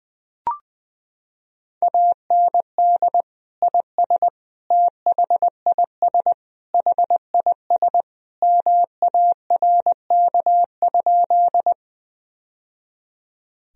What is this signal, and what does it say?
Morse code, heard as a keyed tone, about 20 wpm, 700 Hz. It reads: AND IS THIS HIS MARK?
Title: Sentences from Top 1000 Words - Review Part 1 - 20wpm